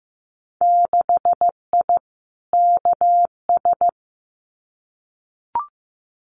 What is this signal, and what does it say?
Morse code, heard as a keyed tone, about 15 wpm, 700 Hz. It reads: 6I KS
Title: Winter Field Day Exchanges Encore 15wpm